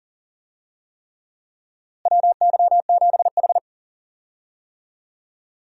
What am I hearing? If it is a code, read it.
Morse code, heard as a keyed tone, about 40 wpm, 700 Hz. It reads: WY7H
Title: USA Call Signs - 4 Characters - Repeated 3x - 40wpm